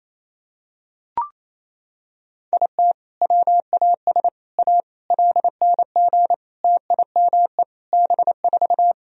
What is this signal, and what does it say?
Morse code, heard as a keyed tone, about 28 wpm, 700 Hz. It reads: IT WAS A LNG TIME B4
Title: Sentences from Top 500 Words and Abbreviations 28wpm